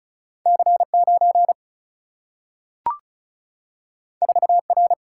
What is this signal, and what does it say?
Morse code, heard as a keyed tone, about 36 wpm, 700 Hz. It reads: C9 4R
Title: Single-Letter Single-Number 35wpm